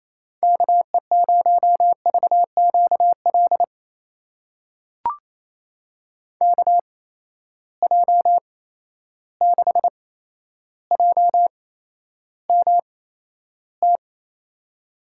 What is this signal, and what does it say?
Morse code, heard as a keyed tone, about 27 wpm, spaced as for 12 wpm, 700 Hz. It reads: KE0VQL KJ6JMT
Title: USA Call Signs - 6 Characters - 8x Character Spacing - 28wpm